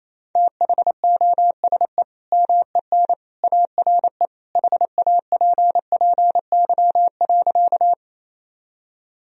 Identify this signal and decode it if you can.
Morse code, heard as a keyed tone, about 28 wpm, 700 Hz. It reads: THOSE MEN ARE HAPPY.
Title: Sentences from Top 1000 Words - Review Part 1 - 28wpm